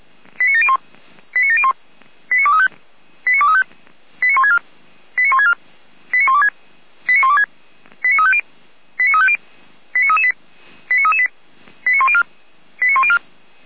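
Five-tone selective call (ZVEI), heard as quick runs of five tones, tones of about 70 ms. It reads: 89701 89701 89235 89235 89164 89164 89126 89126 89350 89350 89338 89338 89104 89104
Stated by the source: ZVEI five-tone call received by SDR.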